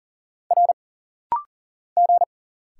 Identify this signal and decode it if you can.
Morse code, heard as a keyed tone, about 43 wpm, 700 Hz. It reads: R G
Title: Single Letter-Number - Rapid-Fire 40wpm